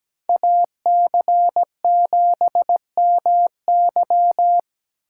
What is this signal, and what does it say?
Morse code, heard as a keyed tone, about 17 wpm, 700 Hz. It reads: AC7MY
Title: USA Call Signs - 5 Characters - 8x Character Spacing - 17wpm